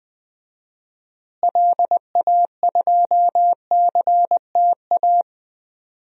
Morse code, heard as a keyed tone, about 20 wpm, 700 Hz. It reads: LA2CTA
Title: DXCC-100 Call Signs - 6 Characters - Repeated 2x - 20wpm